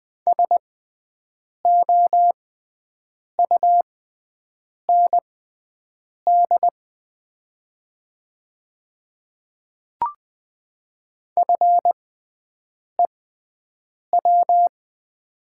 Morse code, heard as a keyed tone, about 20 wpm, spaced as for 7 wpm, 700 Hz. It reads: SOUND FEW
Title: 6x Character Spacing - Top 300 Words - 20wpm